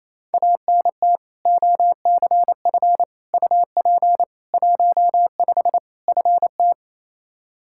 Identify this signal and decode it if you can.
Morse code, heard as a keyed tone, about 27 wpm, 700 Hz. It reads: ANT OCF UP 15 FT
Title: Practice QSO - Exchange 2 - Info - 28wpm